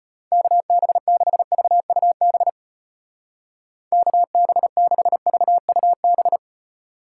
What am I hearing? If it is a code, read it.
Morse code, heard as a keyed tone, about 31 wpm, 700 Hz. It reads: KB6VUB KB6VUB
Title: Speed Racing - USA 2x3 Call Signs 25wpm